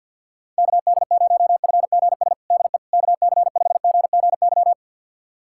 Morse code, heard as a keyed tone, about 52 wpm, 700 Hz. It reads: KD0FGS DE KC5GGY
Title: USA Callsign DE Callsign - 6 Characters - 50wpm